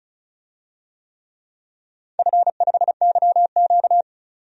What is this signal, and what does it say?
Morse code, heard as a keyed tone, about 36 wpm, 700 Hz. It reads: F5YQ
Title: DXCC-100 Call Signs - 4 Characters - 35wpm